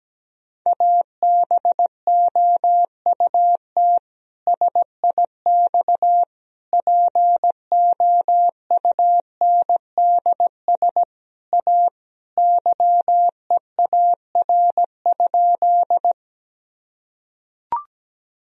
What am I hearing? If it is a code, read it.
Morse code, heard as a keyed tone, about 17 wpm, 700 Hz. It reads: ABOUT SIX POUNDS A YEAR?
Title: Sentences from Top 1000 Words - Review Part 2 - 17wpm